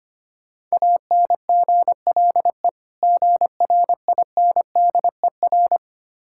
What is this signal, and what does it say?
Morse code, heard as a keyed tone, about 25 wpm, 690 Hz. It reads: ANGLE GRINDER
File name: practice / 25wpm / Difficult Words with Hints - 25wpm